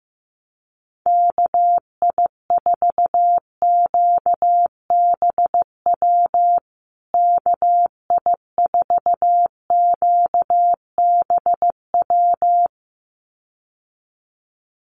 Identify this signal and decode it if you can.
Morse code, heard as a keyed tone, about 15 wpm, 700 Hz. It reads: KI4QBW KI4QBW
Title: USA Call Signs Repeated 2x 15wpm